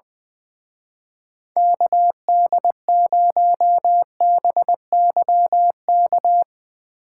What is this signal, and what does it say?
Morse code, heard as a keyed tone, about 21 wpm, 700 Hz. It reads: KD0BYK
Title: Speed Racing - USA 2x3 Call Signs 20wpm